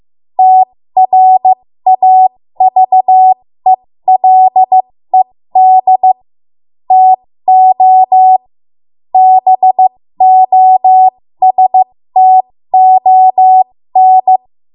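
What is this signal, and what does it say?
Morse code, heard as a keyed tone, about 15 wpm, spaced as for 13 wpm, 750 Hz. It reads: TRAVELED TO BOSTON
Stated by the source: 13 wpm Morse code practice archive transmission.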